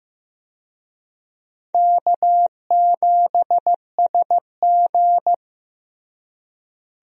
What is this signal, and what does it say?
Morse code, heard as a keyed tone, about 15 wpm, 700 Hz. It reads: K7SG
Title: USA Call Signs - 4 Characters - Repeated 3x - 15wpm